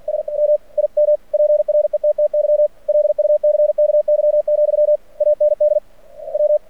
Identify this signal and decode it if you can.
Morse code, heard as a keyed tone, about 46 wpm, 600 Hz. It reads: S M IM PRETTY LUCKY, AND W
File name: archived / CRSnet